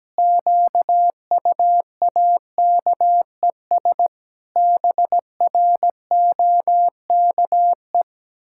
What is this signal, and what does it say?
Morse code, heard as a keyed tone, about 17 wpm, 700 Hz. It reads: QUAKES BROKE
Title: Pangram Challenge v2 - 17wpm